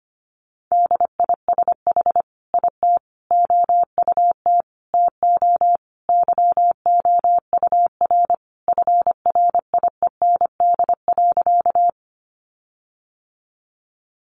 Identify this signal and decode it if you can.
Morse code, heard as a keyed tone, about 25 wpm, 700 Hz. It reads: DISH IT OUT TO YOUR FRIEND.
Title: Sentences from Top 1300 Words 25wpm